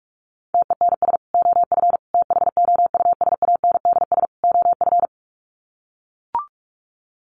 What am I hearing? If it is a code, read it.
Morse code, heard as a keyed tone, about 44 wpm, 700 Hz. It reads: TENS OF THOUSANDS OF
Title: Sets of 4 Words 45wpm